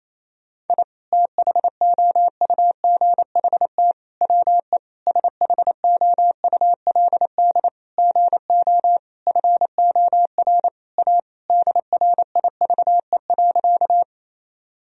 Morse code, heard as a keyed tone, about 28 wpm, 700 Hz. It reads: I THOUGHT WE SHOULD GO FOR A DRIVE.
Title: Sentences from Top 1000 Words - Review Part 2 - 28wpm